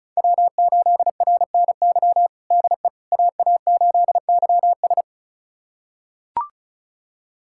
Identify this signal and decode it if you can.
Morse code, heard as a keyed tone, about 34 wpm, 700 Hz. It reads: W8RNY DE AA8YS
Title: USA Callsign DE Callsign 35wpm